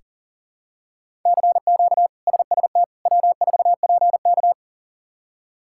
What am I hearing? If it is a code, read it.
Morse code, heard as a keyed tone, about 40 wpm, 700 Hz. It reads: CQ SST W4PK
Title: K1USN SST QSO Practice - 40wpm